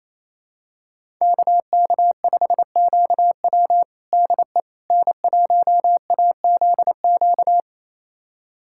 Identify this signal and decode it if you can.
Morse code, heard as a keyed tone, about 28 wpm, 700 Hz. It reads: KK5QW DE N1AZQ
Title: USA Callsign DE Callsign - 5 Characters - 28wpm